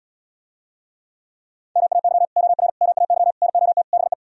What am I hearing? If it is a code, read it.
Morse code, heard as a keyed tone, about 77 wpm, 700 Hz. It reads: MAY OR MAY NOT BE